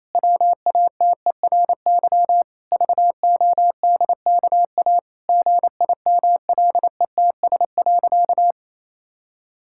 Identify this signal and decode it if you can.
Morse code, heard as a keyed tone, about 28 wpm, 700 Hz. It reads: WATERY VODKA GIMLETS.
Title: Pangram Challenge v2 - 28wpm